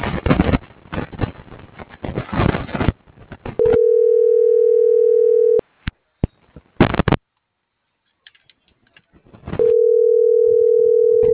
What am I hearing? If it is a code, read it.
Ringback tone (North America), 440 + 480 Hz.